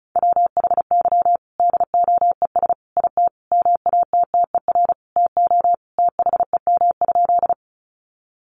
Morse code, heard as a keyed tone, about 34 wpm, 700 Hz. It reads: WHY DOES IT MATTER TO THEM?
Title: Sentences from Top 700 Words 35wpm